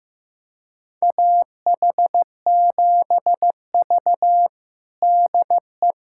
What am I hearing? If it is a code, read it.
Morse code, heard as a keyed tone, about 16 wpm, 700 Hz. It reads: AH7V DE
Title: USA Callsign DE Callsign - 4 Characters - 15wpm